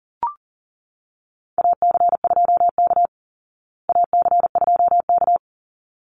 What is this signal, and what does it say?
Morse code, heard as a keyed tone, about 40 wpm, 700 Hz. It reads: AC2K AC2K
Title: Call Signs Repeated 3x - 40wpm